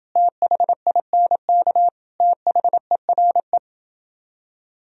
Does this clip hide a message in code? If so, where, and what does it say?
Yes: Morse code from 0.2 to 3.6 s, heard as a keyed tone, about 29 wpm, 700 Hz. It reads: THINK THERE